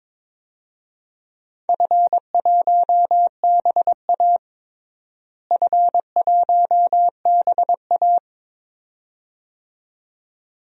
Morse code, heard as a keyed tone, about 22 wpm, 700 Hz. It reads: F1BA F1BA
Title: DXCC-100 Call Signs - 4 Characters - Repeated 2x - 22wpm